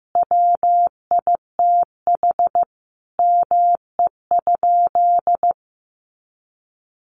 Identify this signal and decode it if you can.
Morse code, heard as a keyed tone, about 15 wpm, 700 Hz. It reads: WITH ME?
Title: Sentences from Top 300 Words 15wpm